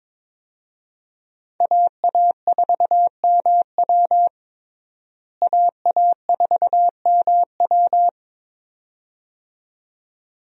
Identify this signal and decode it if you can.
Morse code, heard as a keyed tone, about 22 wpm, 700 Hz. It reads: AA4MW AA4MW
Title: USA Call Signs - 5 Characters - Repeated 2x - 22wpm